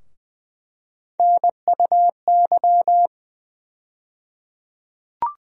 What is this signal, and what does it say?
Morse code, heard as a keyed tone, about 20 wpm, 700 Hz. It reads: NUY
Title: USA Call Sign Postfixes - 3 Letters 20wpm